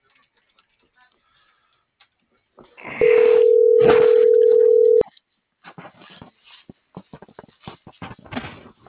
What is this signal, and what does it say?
Ringback tone (North America), 440 + 480 Hz.